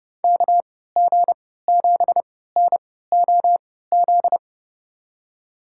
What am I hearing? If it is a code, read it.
Morse code, heard as a keyed tone, about 30 wpm, spaced as for 17 wpm, 700 Hz. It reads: KG7NOZ
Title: USA Call Signs - 6 Characters - 3x Character Spacing - 30wpm